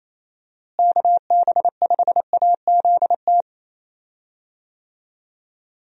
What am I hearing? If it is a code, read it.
Morse code, heard as a keyed tone, about 28 wpm, 700 Hz. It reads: KB5AZT
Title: USA Call Signs - 6 Characters - Repeated 3x - 28wpm